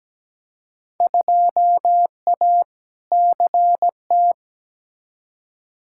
Morse code, heard as a keyed tone, about 17 wpm, 700 Hz. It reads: 2A CT
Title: Field Day Exchanges - 17wpm